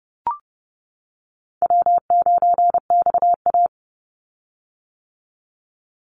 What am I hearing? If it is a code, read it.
Morse code, heard as a keyed tone, about 31 wpm, 700 Hz. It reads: W9XA
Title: Easy Call Signs - 30wpm